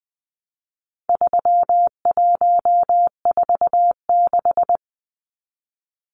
Morse code, heard as a keyed tone, about 20 wpm, 700 Hz. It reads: 3146
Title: Numbers - 4-Digit 20wpm